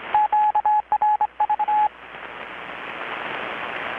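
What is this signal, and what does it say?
Morse code, heard as a keyed tone, about 25 wpm, 850 Hz, with radio noise. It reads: QRV
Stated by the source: Morse from KPH's Night of Nights XX maritime radiotelegraphy event, received in Chiapas, Mexico about 3,800 km away on a Kenwood R-600 with a grounded 10 m long-wire antenna.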